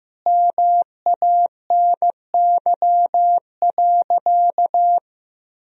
Morse code, heard as a keyed tone, about 15 wpm, 700 Hz. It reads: MANY.
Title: Sentences from Top 1000 Words - Review Part 1 - 15wpm